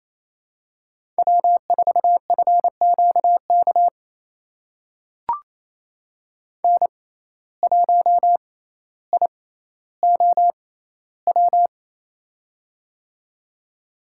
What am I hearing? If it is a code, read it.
Morse code, heard as a keyed tone, about 28 wpm, spaced as for 14 wpm, 700 Hz. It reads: W4FQK N1IOW
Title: USA Call Signs - 6x Character Spacing - 28wpm